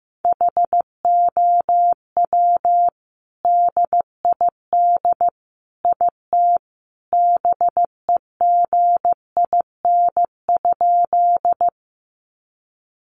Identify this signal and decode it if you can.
Morse code, heard as a keyed tone, about 15 wpm, 700 Hz. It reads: HOW DID IT BEGIN?
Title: Sentences from Top 300 Words 15wpm